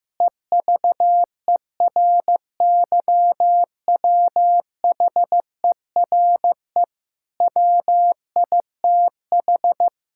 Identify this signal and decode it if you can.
Morse code, heard as a keyed tone, about 15 wpm, 700 Hz. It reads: EVERYWHERE WITH
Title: Sentences from Top 1600 Words 15wpm